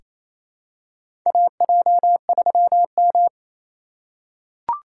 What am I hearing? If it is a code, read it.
Morse code, heard as a keyed tone, about 28 wpm, 700 Hz. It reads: AJ3M
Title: USA Call Signs - 4 Characters - 4x Character Spacing - 28wpm